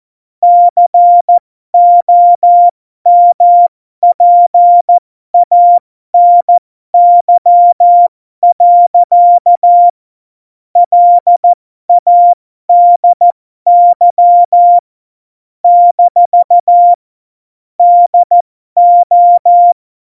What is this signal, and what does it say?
Morse code, heard as a keyed tone, about 14 wpm, spaced as for 12 wpm, 700 Hz. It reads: COMPANY. LADY - DO
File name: Morsekurs / Witze